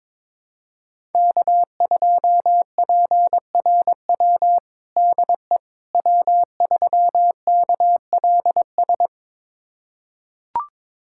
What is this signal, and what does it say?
Morse code, heard as a keyed tone, about 22 wpm, 700 Hz. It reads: K2PRW DE W3KLS
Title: USA Callsign DE Callsign - 5 Characters - 22wpm